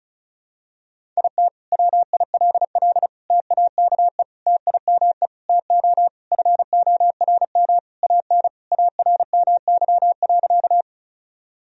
Morse code, heard as a keyed tone, about 36 wpm, 700 Hz. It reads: IT WILL TAKE TIME TO FORM AN ARMY.
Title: Sentences from Top 1000 Words - Review Part 1 - 35wpm